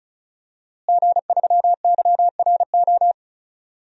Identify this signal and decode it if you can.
Morse code, heard as a keyed tone, about 36 wpm, 700 Hz. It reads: G3YRO
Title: DXCC-100 Call Signs - 5 Characters - Repeated 3x - 35wpm